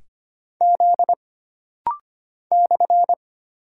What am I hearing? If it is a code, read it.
Morse code, heard as a keyed tone, about 25 wpm, 700 Hz. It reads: Z /
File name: Single Character (Letter:Number:Punctuation:Prosign) - Rapid-Fire - 25wpm